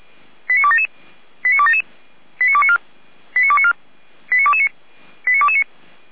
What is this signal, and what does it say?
Five-tone selective call (ZVEI), heard as quick runs of five tones, tones of about 70 ms. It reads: 89277 89277 89294 89294 89229 89229